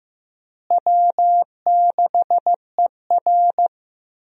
Morse code, heard as a keyed tone, about 16 wpm, 700 Hz. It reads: W6ER
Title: USA Call Signs - 4 Characters - Repeated 2x - 15wpm